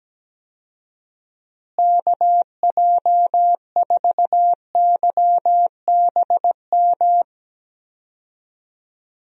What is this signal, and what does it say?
Morse code, heard as a keyed tone, about 17 wpm, 700 Hz. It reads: KJ4YBM
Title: USA Call Signs - 6 Characters - Repeated 2x - 17wpm